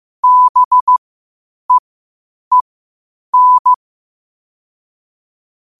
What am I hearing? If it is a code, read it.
Morse code, heard as a keyed tone, about 15 wpm, spaced as for 8 wpm, 1000 Hz. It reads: BEEN
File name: qotdfiles / 8wpm